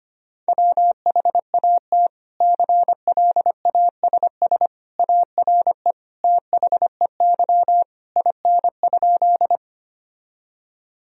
Morse code, heard as a keyed tone, about 25 wpm, 700 Hz. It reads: WHAT CLASS ARE THEY IN?
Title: Sentences from Top 1000 Words - Review Part 2 - 25wpm